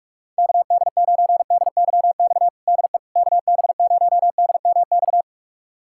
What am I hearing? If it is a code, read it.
Morse code, heard as a keyed tone, about 46 wpm, 700 Hz. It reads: KD9DYX DE KB0DMX
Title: USA Callsign DE Callsign 45wpm